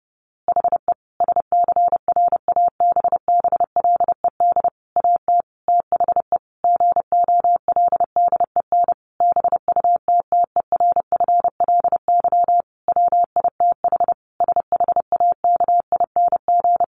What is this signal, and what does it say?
Morse code, heard as a keyed tone, about 30 wpm, 700 Hz. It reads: HE SCRABBLED AT THE GOLDEN BUTTERFLY WITH SHAKING
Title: A Sound Of Thunder 30wpm